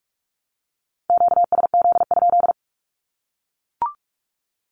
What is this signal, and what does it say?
Morse code, heard as a keyed tone, about 44 wpm, 700 Hz. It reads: QSZ?
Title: Q-Codes 45wpm